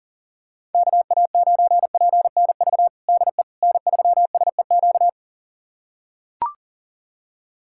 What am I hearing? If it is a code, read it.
Morse code, heard as a keyed tone, about 37 wpm, 700 Hz. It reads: KA9PNV DE N3SEQ
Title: USA Callsign DE Callsign 40wpm